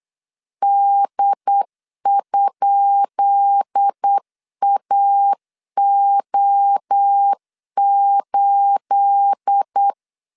Morse code, heard as a keyed tone, about 8 wpm, 800 Hz. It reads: D?AO8
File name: LEZIONI / 10 SIGNS / 10B INT 8wpm 3534